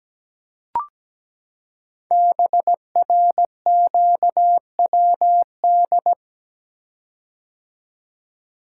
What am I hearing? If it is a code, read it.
Morse code, heard as a keyed tone, about 17 wpm, 700 Hz. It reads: BRQWD
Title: Code Groups - 5 Letters - 17wpm